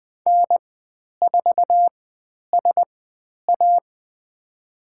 Morse code, heard as a keyed tone, about 20 wpm, spaced as for 10 wpm, 700 Hz. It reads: N4SA
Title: 095 Lesson 16g - Letter c - Call Signs 20wpm